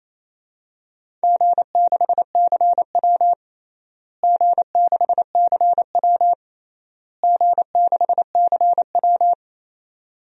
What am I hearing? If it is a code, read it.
Morse code, heard as a keyed tone, about 28 wpm, 700 Hz. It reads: G6CW G6CW G6CW
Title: DXCC-100 Call Signs - 4 Characters - Repeated 3x - 28wpm